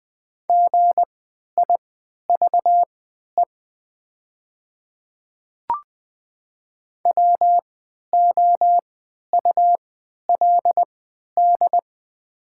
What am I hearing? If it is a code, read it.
Morse code, heard as a keyed tone, about 20 wpm, spaced as for 11 wpm, 700 Hz. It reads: GIVE WOULD
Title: 3x Character Spacing - Top 200 Words - 20wpm